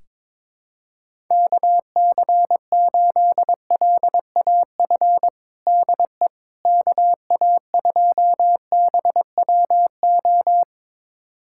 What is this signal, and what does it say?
Morse code, heard as a keyed tone, about 22 wpm, 700 Hz. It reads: KC8LAF DE KA2BWO
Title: USA Callsign DE Callsign - 6 Characters - 22wpm